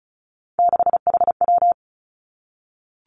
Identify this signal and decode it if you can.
Morse code, heard as a keyed tone, about 34 wpm, 700 Hz. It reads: 6HW
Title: Three Challenging Letter-Number Combinations 35wpm